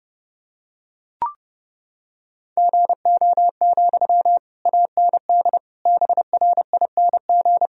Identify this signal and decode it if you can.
Morse code, heard as a keyed tone, about 29 wpm, 700 Hz. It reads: GO, AND BRING
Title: Sentences from Top 1000 Words - Review Part 3 - 30wpm